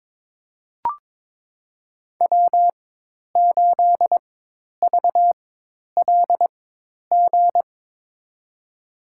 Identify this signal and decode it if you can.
Morse code, heard as a keyed tone, about 21 wpm, spaced as for 10 wpm, 700 Hz. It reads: W8VLG
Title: USA Call Signs - 5 Characters - 4x Character Spacing - 22wpm